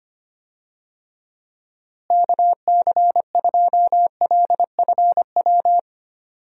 Morse code, heard as a keyed tone, about 25 wpm, 700 Hz. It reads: KC2LFW
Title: USA Call Signs - 6 Characters - Repeated 2x - 25wpm